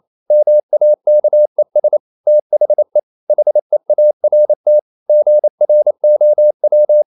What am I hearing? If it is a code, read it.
Morse code, heard as a keyed tone, about 28 wpm, 600 Hz. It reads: MAKES THE HEART GROW
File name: CW / Practice / EnglishSayings28